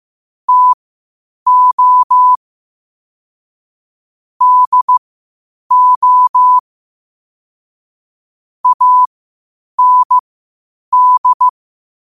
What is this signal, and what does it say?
Morse code, heard as a keyed tone, about 15 wpm, spaced as for 8 wpm, 1000 Hz. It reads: TO DO AND
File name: qotdfiles / 8wpm